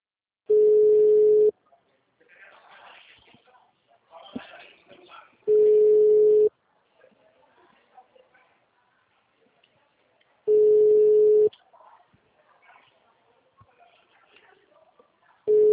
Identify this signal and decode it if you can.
Ringback tone (Europe, 425 Hz), with speech.